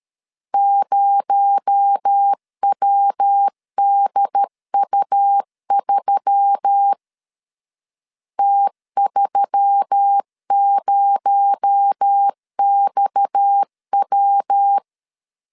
Morse code, heard as a keyed tone, about 13 wpm, 800 Hz. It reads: 0WDU3 T30XW
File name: LEZIONI / 6 PKX0 / 6 PKX0 12wpm 2231